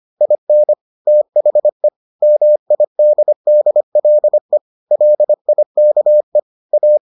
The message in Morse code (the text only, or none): IN THE MIDDLE LIKE A